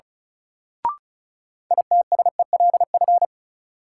Morse code, heard as a keyed tone, about 36 wpm, 700 Hz. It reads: ITSELF
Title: Sentences from Top 800 Words - New Word Drill - 35wpm